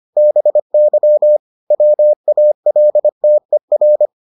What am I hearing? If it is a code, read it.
Morse code, heard as a keyed tone, about 25 wpm, 600 Hz. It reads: BY WALTER